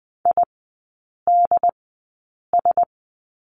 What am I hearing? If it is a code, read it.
Morse code, heard as a keyed tone, about 19 wpm, spaced as for 8 wpm, 700 Hz. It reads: IDS
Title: 067 Lesson 12e - Letter d - Three Characters 20wpm